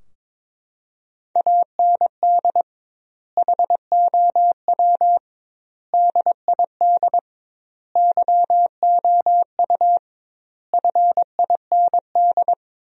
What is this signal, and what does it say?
Morse code, heard as a keyed tone, about 23 wpm, 700 Hz. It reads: AND HOW DID YOU FIND
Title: 2x Word-Spacing - Sentences from Top 200 Words - 22wpm